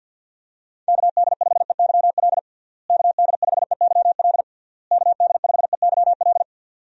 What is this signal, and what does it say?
Morse code, heard as a keyed tone, about 50 wpm, 700 Hz. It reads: KD5EYL KD5EYL KD5EYL
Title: USA Call Signs - 6 Characters - Repeated 3x - 50wpm